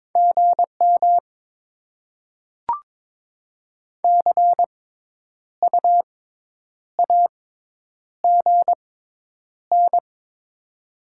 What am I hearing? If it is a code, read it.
Morse code, heard as a keyed tone, about 22 wpm, spaced as for 9 wpm, 700 Hz. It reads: GM CU AGN
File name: Top 20 QSO Elements - 6x Character Spacing - 22wpm